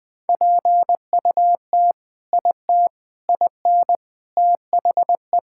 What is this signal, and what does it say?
Morse code, heard as a keyed tone, about 20 wpm, 700 Hz. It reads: PUT IT IN THE
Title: Sentences with 3-Word Phrases - 20wpm